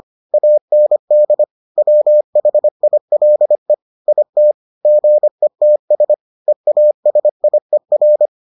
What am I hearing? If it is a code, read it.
Morse code, heard as a keyed tone, about 25 wpm, 600 Hz. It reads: AND WHILE IT GETS EASIER